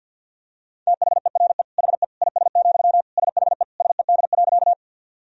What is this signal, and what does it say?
Morse code, heard as a keyed tone, about 52 wpm, 700 Hz. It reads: THERE HE IS, SHE SED.
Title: Sentences from Top 100 Words and Abbreviations 50wpm